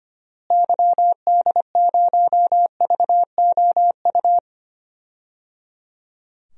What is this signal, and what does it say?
Morse code, heard as a keyed tone, about 25 wpm, 700 Hz. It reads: YD0VOU
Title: DXCC-100 Call Signs - 6 Characters - Repeated 3x - 25wpm